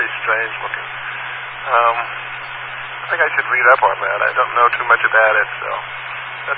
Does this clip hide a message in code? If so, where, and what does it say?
Yes: Emergency Alert System (SAME) tones (header bursts) from 0.9 to 5.5 s, heard as harsh buzzing bursts, with speech.